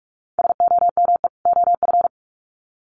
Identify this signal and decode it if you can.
Morse code, heard as a keyed tone, about 44 wpm, 700 Hz. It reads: SOME OF
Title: Sets of 2 Words 45wpm